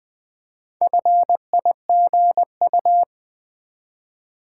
Morse code, heard as a keyed tone, about 20 wpm, 700 Hz. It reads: FIGU
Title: Build A Word - 2 Letters At A Time - 6-Letter Words - 20wpm